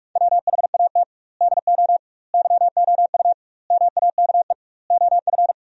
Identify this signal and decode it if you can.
Morse code, heard as a keyed tone, about 46 wpm, 700 Hz. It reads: WHAT DO YOU MAKE OF
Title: 2x Word-Spacing - Sentences from Top 100 Words - 45wpm